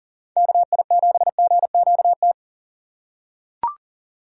Morse code, heard as a keyed tone, about 40 wpm, 700 Hz. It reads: KI7GQT
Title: USA Call Signs - 2x Character Spacing - 40wpm